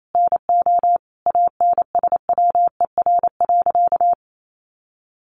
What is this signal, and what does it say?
Morse code, heard as a keyed tone, about 28 wpm, 700 Hz. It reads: NO ANSWER.
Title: Sentences from Top 500 Words 28wpm